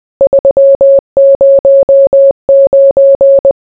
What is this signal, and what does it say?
Morse code, heard as a keyed tone, about 20 wpm, 550 Hz. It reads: 309